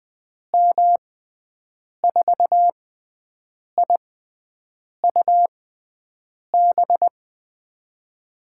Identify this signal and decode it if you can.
Morse code, heard as a keyed tone, about 20 wpm, spaced as for 7 wpm, 700 Hz. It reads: M4IUB
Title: USA Call Signs - 6 Characters - 6x Character Spacing - 20wpm